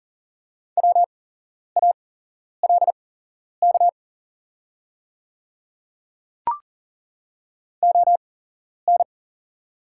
Morse code, heard as a keyed tone, about 40 wpm, spaced as for 11 wpm, 700 Hz. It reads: WALK ON
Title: 8x Character Spacing - Top 300 Words - 40wpm